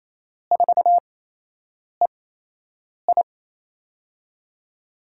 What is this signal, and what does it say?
Morse code, heard as a keyed tone, about 27 wpm, spaced as for 8 wpm, 700 Hz. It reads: 4EI
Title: USA Call Signs - 5 Characters - 8x Character Spacing - 28wpm